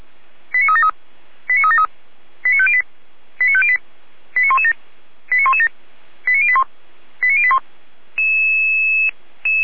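Five-tone selective call (ZVEI), heard as quick runs of five tones, tones of about 70 ms. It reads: 89373 89373 89508 89508 89117 89117 89081 89081
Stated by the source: ZVEI five-tone selective call received with an SDR.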